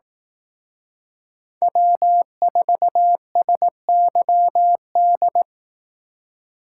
Morse code, heard as a keyed tone, about 18 wpm, 700 Hz. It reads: W4SYD
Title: Speed Racing - USA 1x3 Call Signs 15wpm